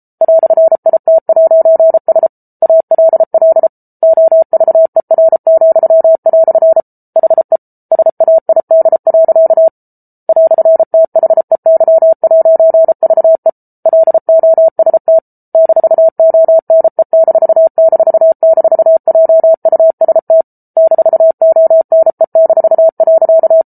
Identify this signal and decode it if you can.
Morse code, heard as a keyed tone, about 34 wpm, 650 Hz. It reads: "IT'S ALL OVER," HE SAID. "THEY'VE LOST -ONE---JUST -ONE-.